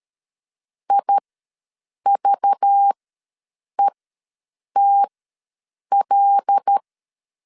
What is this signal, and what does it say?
Morse code, heard as a keyed tone, about 13 wpm, spaced as for 7 wpm, 800 Hz. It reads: IVETL
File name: LEZIONI / 5 LBVC / 5 LBV ws 12wpm 3015